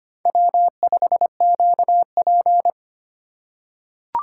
Morse code, heard as a keyed tone, about 25 wpm, 700 Hz. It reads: W5QP
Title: K1USN SST QSO Practice - 25wpm